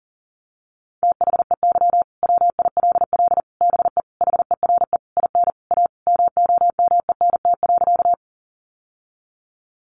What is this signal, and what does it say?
Morse code, heard as a keyed tone, about 40 wpm, 700 Hz. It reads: THEY WILL BE HERE IN A MOMENT.
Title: Sentences from Top 1000 Words 40wpm